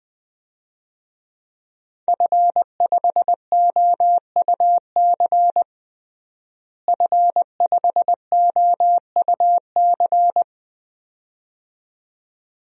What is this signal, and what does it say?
Morse code, heard as a keyed tone, about 21 wpm, 700 Hz. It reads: F5OUC F5OUC
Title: DXCC-100 Call Signs - 5 Characters - Repeated 2x - 20wpm